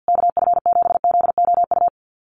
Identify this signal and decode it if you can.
Morse code, heard as a keyed tone, about 50 wpm, 700 Hz. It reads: KF7ZOU
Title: USA Call Signs Repeated 3x 50wpm